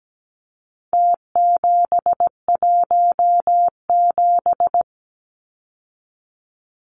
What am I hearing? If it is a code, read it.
Morse code, heard as a keyed tone, about 17 wpm, 700 Hz. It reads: T717
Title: Numbers - Contest Style - 4-Digit 17wpm